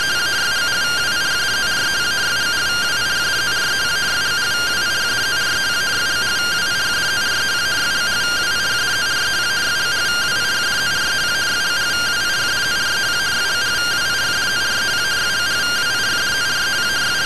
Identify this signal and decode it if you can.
Two-tone teleprinter signal, about 170 Hz shift.